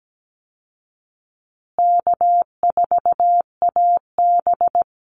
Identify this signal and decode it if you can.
Morse code, heard as a keyed tone, about 17 wpm, 700 Hz. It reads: K4AB
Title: Call Signs Repeated 2x - 17wpm